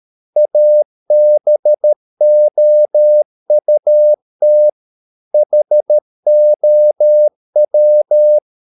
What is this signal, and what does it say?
Morse code, heard as a keyed tone, about 13 wpm, 600 Hz. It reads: ABOUT HOW